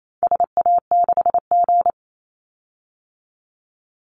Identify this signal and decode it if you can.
Morse code, heard as a keyed tone, about 28 wpm, 700 Hz. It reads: SA6G
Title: Call Signs Repeated 3x - 28wpm